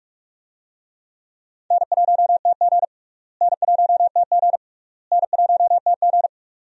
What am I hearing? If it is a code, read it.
Morse code, heard as a keyed tone, about 46 wpm, 700 Hz. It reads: N1TG N1TG N1TG
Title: USA Call Signs - 4 Characters - Repeated 3x - 45wpm